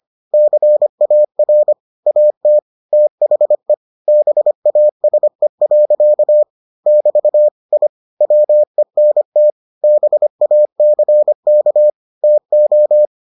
Morse code, heard as a keyed tone, about 25 wpm, 600 Hz. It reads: CAR AT THE BASE. <BT> I WENT BACK TO